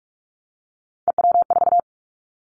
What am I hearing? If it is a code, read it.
Morse code, heard as a keyed tone, about 44 wpm, 700 Hz. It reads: EW4
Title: Callsign Prefixes for Top 100 DXCC - 45wpm